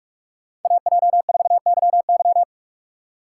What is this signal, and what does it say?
Morse code, heard as a keyed tone, about 45 wpm, 700 Hz. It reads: AJ4YY
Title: USA Call Signs - 6x Character Spacing - 45wpm